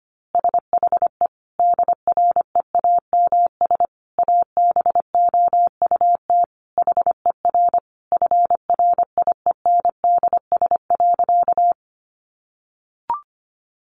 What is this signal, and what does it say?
Morse code, heard as a keyed tone, about 25 wpm, 700 Hz. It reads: SHE DREAMS ABOUT HER FRIENDS.
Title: Easy Phrases 25wpm